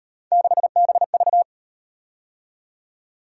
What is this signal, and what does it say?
Morse code, heard as a keyed tone, about 37 wpm, 700 Hz. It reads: 6BV